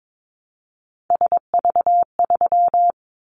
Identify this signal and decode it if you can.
Morse code, heard as a keyed tone, about 22 wpm, 700 Hz. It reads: SV3
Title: Callsign Prefixes for Top 50 DXCC - 22wpm